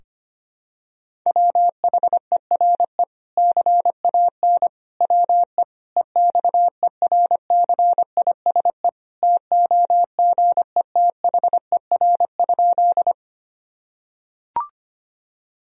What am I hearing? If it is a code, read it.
Morse code, heard as a keyed tone, about 25 wpm, 700 Hz. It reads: WHERE CAN WE EXERCISE TOGETHER?
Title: Sentences from Top 1000 Words - Review Part 4 - 25wpm